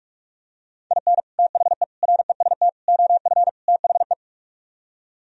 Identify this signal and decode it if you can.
Morse code, heard as a keyed tone, about 47 wpm, 700 Hz. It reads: IN THE REST OF THE